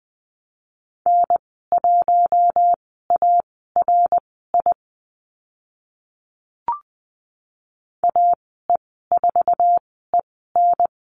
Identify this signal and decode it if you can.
Morse code, heard as a keyed tone, about 20 wpm, spaced as for 14 wpm, 700 Hz. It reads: N1ARI AE4EN
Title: 062 Lesson 11g - Letter h - Call Signs 20wpm